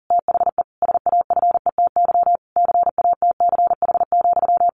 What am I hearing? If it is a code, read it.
Morse code, heard as a keyed tone, about 40 wpm, 700 Hz. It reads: THE SAFETY CATCH,
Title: A Sound Of Thunder 40wpm